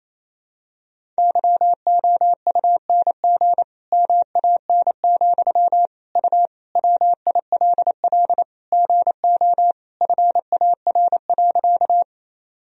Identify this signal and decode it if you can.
Morse code, heard as a keyed tone, about 28 wpm, 700 Hz. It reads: YOUNG MAN, U WILL GO FAR.
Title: Sentences from Top 400 Words and Abbreviations 28wpm